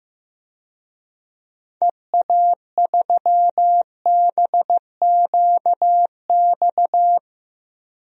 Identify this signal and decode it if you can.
Morse code, heard as a keyed tone, about 16 wpm, 700 Hz. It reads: EA3BQX
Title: DXCC-100 Call Signs - 6 Characters - 15wpm